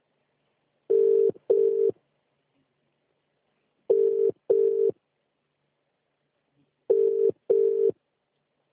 Ringback tone (UK), 400 + 450 Hz.